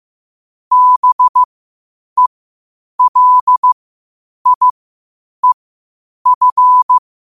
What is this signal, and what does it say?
Morse code, heard as a keyed tone, about 15 wpm, spaced as for 8 wpm, 1000 Hz. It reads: BELIEF
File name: qotdfiles / 8wpm